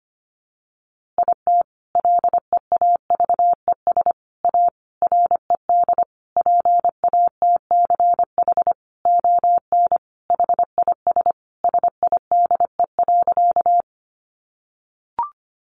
Morse code, heard as a keyed tone, about 25 wpm, 700 Hz. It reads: IT LEAVES A RED PATCH ON HIS SIDE.
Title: A Sound Of Thunder 25wpm